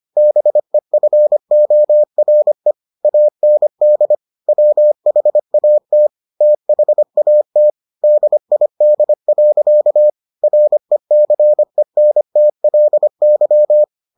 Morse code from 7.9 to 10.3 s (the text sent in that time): DID.